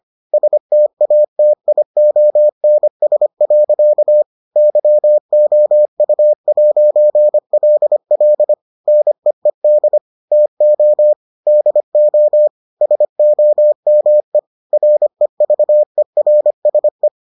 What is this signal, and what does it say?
Morse code, heard as a keyed tone, about 25 wpm, 600 Hz. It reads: STATIONS. YOU'LL NEED TO DO SOME REVERSE